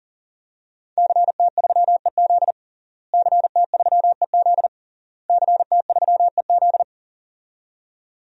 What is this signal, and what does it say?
Morse code, heard as a keyed tone, about 40 wpm, 700 Hz. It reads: CT3EZ CT3EZ CT3EZ
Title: DXCC-100 Call Signs - 5 Characters - Repeated 3x - 40wpm